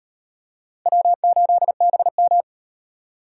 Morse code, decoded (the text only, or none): W8BM